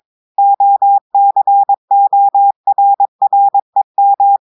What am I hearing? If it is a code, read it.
Morse code, heard as a keyed tone, about 22 wpm, 800 Hz. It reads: OCORREM
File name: audios / portugais / portugais texte en clair 2 vitesse 22 mots minutes